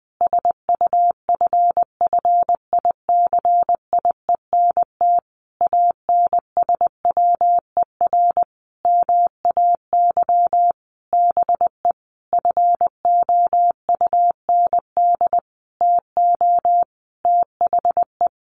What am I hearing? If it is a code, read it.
Morse code, heard as a keyed tone, about 20 wpm, 700 Hz. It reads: SUFFICIENT ANSWER MAY BE FOUND TO THE
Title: Sentences from Top 1400 Words 20wpm